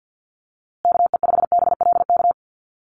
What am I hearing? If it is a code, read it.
Morse code, heard as a keyed tone, about 50 wpm, 700 Hz. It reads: KE5BLK
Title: USA Call Signs Repeated 2x 50wpm